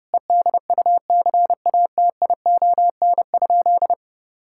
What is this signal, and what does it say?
Morse code, heard as a keyed tone, about 32 wpm, 700 Hz. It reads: EDUCATION?
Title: Sentences from Top 1000 Words - Review Part 6 - 30wpm